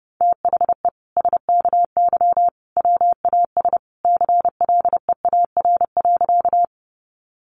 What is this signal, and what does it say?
Morse code, heard as a keyed tone, about 28 wpm, 700 Hz. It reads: THE SKY WAS CLEAR.
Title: Sentences from Top 1200 Words 30wpm